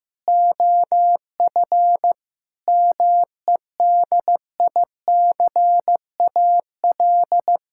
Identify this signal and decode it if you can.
Morse code, heard as a keyed tone, about 15 wpm, 700 Hz. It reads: OF MEDICAL
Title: Sentences from Top 1000 Words - Review Part 6 - 15wpm